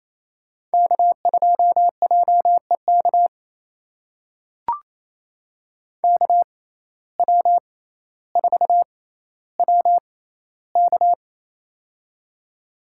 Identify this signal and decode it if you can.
Morse code, heard as a keyed tone, about 28 wpm, spaced as for 14 wpm, 700 Hz. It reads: K2JEK KW4WK
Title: USA Call Signs - 6x Character Spacing - 28wpm